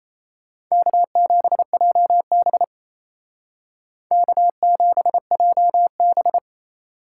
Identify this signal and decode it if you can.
Morse code, heard as a keyed tone, about 35 wpm, 700 Hz. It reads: K7JB K7JB
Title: Speed Racing - USA 1x2 Call Signs 28wpm